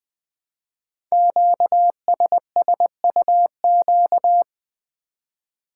Morse code, heard as a keyed tone, about 20 wpm, 700 Hz. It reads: QSSUQ
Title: Code Groups - 5 Letters - 20wpm